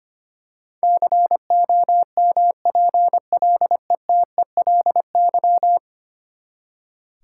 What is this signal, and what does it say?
Morse code, heard as a keyed tone, about 25 wpm, 700 Hz. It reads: COMPLETELY
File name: practice / 25wpm / Ten Letter Words - 25wpm